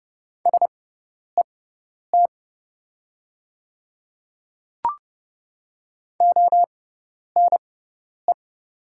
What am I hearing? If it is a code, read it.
Morse code, heard as a keyed tone, about 30 wpm, spaced as for 10 wpm, 700 Hz. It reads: SET ONE
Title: 6x Character Spacing - Top 300 Words - 30wpm